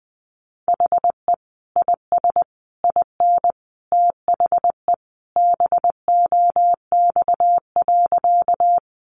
Morse code, heard as a keyed tone, about 20 wpm, 700 Hz. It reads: HE IS IN THE BOX.
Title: Sentences from Top 500 Words 20wpm